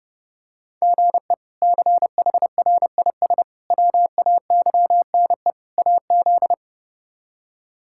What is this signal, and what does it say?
Morse code, heard as a keyed tone, about 32 wpm, 700 Hz. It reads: GE CHRIS WAYNE AZ
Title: K1USN SST QSO Practice - 30wpm